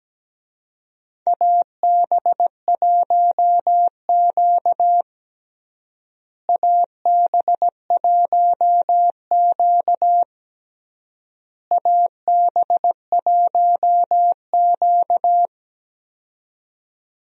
Morse code, heard as a keyed tone, about 17 wpm, 700 Hz. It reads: AB1Q AB1Q AB1Q
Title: USA Call Signs - 4 Characters - Repeated 3x - 17wpm